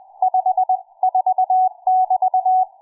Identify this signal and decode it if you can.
Morse code, heard as a keyed tone, about 20 wpm, 740 Hz. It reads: 54<BT>
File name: cw forum / fm5cd 1833 le17avril08 06h51